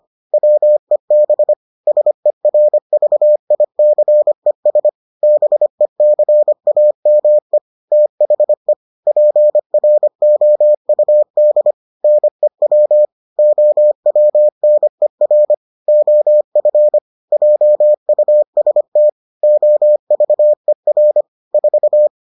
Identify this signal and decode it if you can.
Morse code, heard as a keyed tone, about 25 wpm, 600 Hz. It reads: WEB SERVICES BECAME THE PROUD NEW OWNER OF JUST OVER 4